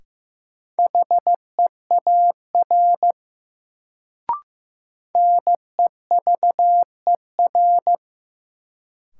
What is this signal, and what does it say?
Morse code, heard as a keyed tone, about 15 wpm, 700 Hz. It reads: HEAR NEVER
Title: Sentences from Top 300 Words - New Word Drill - 15wpm